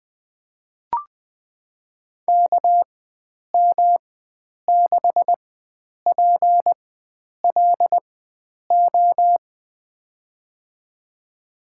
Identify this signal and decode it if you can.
Morse code, heard as a keyed tone, about 20 wpm, spaced as for 9 wpm, 700 Hz. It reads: KM6PLO
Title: USA Call Signs - 4x Character Spacing - 20wpm